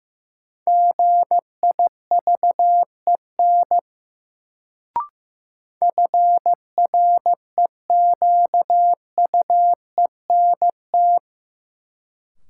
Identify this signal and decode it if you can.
Morse code, heard as a keyed tone, about 15 wpm, 700 Hz. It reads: GIVEN FREQUENT
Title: Sentences from Top 1400 Words - New Word Drill - 15wpm